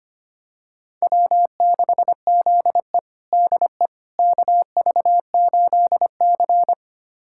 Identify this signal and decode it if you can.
Morse code, heard as a keyed tone, about 25 wpm, 700 Hz. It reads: W6ZE DE KV8C
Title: USA Callsign DE Callsign - 4 Characters - 25wpm